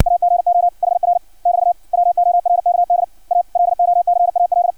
Morse code, heard as a keyed tone, about 60 wpm, 700 Hz. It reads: AGO HR <BT> WORKD A LOCAL
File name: archived / CRSnet